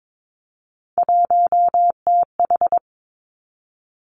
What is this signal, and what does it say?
Morse code, heard as a keyed tone, about 22 wpm, 700 Hz. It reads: 1TH
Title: Three Challenging Letter-Number Combinations 22wpm